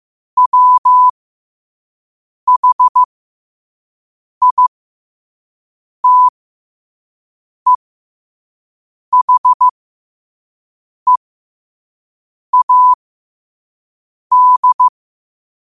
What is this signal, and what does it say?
Morse code, heard as a keyed tone, about 15 wpm, spaced as for 5 wpm, 1000 Hz. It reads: WHITEHEAD
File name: qotdfiles / 5wpm